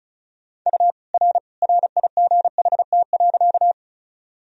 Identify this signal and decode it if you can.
Morse code, heard as a keyed tone, about 36 wpm, 700 Hz. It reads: U R RIGHT.
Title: Sentences from Top 200 Words and Abbreviations 35wpm